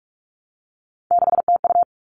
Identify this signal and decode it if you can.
Morse code, heard as a keyed tone, about 44 wpm, 700 Hz. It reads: 6TU
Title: All Cut Numbers - 3-Digit 45wpm